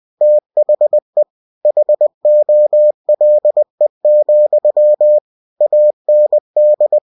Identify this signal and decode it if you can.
Morse code, heard as a keyed tone, about 20 wpm, 600 Hz. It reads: THE HOLE, AND
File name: cw-books / winnie-the-pooh / winnie-20wpm-page-0013